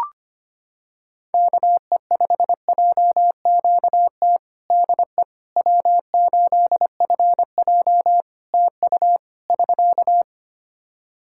Morse code, heard as a keyed tone, about 25 wpm, 700 Hz. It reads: KE5JQT DE W8FJ TU <SK>
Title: Practice QSO - Exchange 4 - Ending - 25wpm